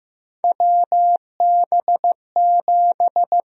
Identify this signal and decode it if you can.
Morse code, heard as a keyed tone, about 15 wpm, 700 Hz. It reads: WB7
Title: USA Call Sign Prefixes - Repeated 2x - 15wpm